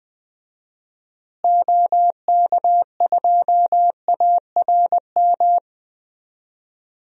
Morse code, heard as a keyed tone, about 20 wpm, 700 Hz. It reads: OK2ARM
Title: DXCC-100 Call Signs - 6 Characters - Repeated 3x - 20wpm